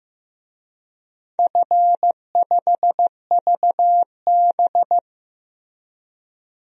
Morse code, heard as a keyed tone, about 16 wpm, 700 Hz. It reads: F5VB
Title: DXCC-100 Call Signs - 4 Characters - Repeated 3x - 15wpm